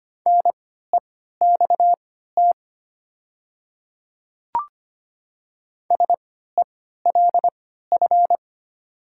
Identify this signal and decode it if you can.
Morse code, heard as a keyed tone, about 25 wpm, spaced as for 14 wpm, 700 Hz. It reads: NEXT SELF
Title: 3x Character Spacing - Top 300 Words - 25wpm